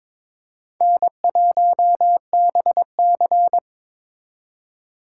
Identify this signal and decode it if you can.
Morse code, heard as a keyed tone, about 26 wpm, 700 Hz. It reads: N1BC
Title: Speed Racing - USA 1x2 Call Signs 22wpm